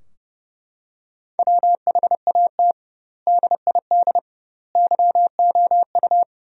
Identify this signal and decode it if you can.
Morse code, heard as a keyed tone, about 32 wpm, 700 Hz. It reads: WHAT DID YOU
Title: 2x Word-Spacing - Sentences from Top 200 Words - 30wpm